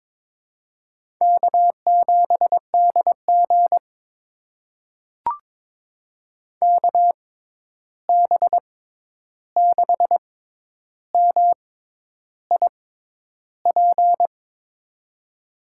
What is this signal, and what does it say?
Morse code, heard as a keyed tone, about 22 wpm, spaced as for 10 wpm, 700 Hz. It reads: K7DG KB6MIP
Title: USA Call Signs - 6x Character Spacing - 22wpm